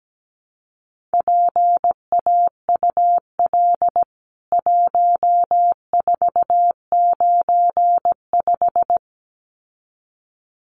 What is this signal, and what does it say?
Morse code, heard as a keyed tone, about 17 wpm, 700 Hz. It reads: PAUL 1495
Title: CWT Exchange 17wpm